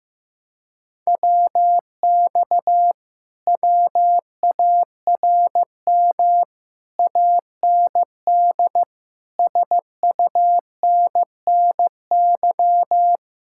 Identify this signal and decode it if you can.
Morse code, heard as a keyed tone, about 16 wpm, 700 Hz. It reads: WX WARM AND SUNNY
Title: Speed-Racing Common QSO Phrases Encore - 15wpm